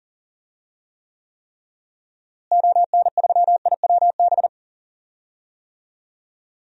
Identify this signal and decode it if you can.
Morse code, heard as a keyed tone, about 40 wpm, 700 Hz. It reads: ON3IWB
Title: DXCC-100 Call Signs - 6 Characters - Repeated 3x - 40wpm